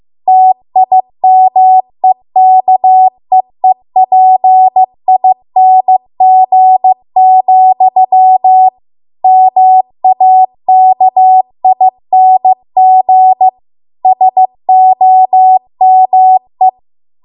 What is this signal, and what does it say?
Morse code, heard as a keyed tone, about 15 wpm, 750 Hz. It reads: TIMEKEEPING, MAKING SOME
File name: Archive / 15 WPM / 15WPM